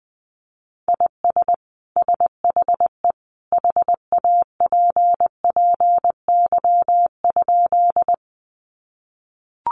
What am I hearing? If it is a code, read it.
Morse code, heard as a keyed tone, about 20 wpm, 700 Hz. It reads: IS SHE HAPPY?
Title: Sentences from Top 1000 Words 20wpm